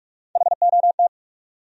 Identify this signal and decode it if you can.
Morse code, heard as a keyed tone, about 43 wpm, 700 Hz. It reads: HOT
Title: Top 100 Words 45wpm